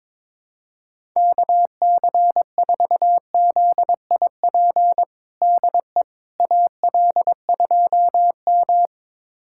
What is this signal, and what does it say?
Morse code, heard as a keyed tone, about 22 wpm, 700 Hz. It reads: KC4ZIP DE AL2M
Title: USA Callsign DE Callsign 22wpm